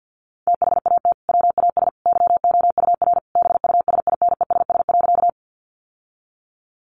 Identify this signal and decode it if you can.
Morse code, heard as a keyed tone, about 50 wpm, 700 Hz. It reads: THAT WAS YOUR BUSINESS.
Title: Sentences from Top 600 Words 50wpm